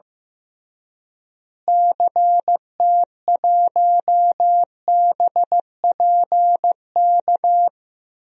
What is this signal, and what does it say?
Morse code, heard as a keyed tone, about 15 wpm, 700 Hz. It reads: CT1BPK
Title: DXCC-100 Call Signs - 6 Characters - Repeated 3x - 15wpm